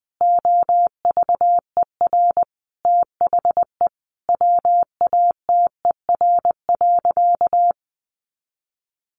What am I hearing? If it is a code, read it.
Morse code, heard as a keyed tone, about 20 wpm, 700 Hz. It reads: OVER THE WATER.
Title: Sentences from Top 300 Words 20wpm